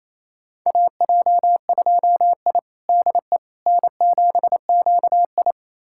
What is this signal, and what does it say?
Morse code, heard as a keyed tone, about 28 wpm, 700 Hz. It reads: AJ2I DE N7QI
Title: USA Callsign DE Callsign - 4 Characters - 28wpm